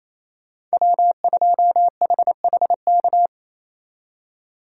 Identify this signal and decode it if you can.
Morse code, heard as a keyed tone, about 36 wpm, 700 Hz. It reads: W2HHK